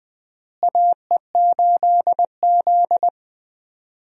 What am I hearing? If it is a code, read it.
Morse code, heard as a keyed tone, about 20 wpm, 700 Hz. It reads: AE8Z
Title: USA Call Signs - 4 Characters - Repeated 3x - 20wpm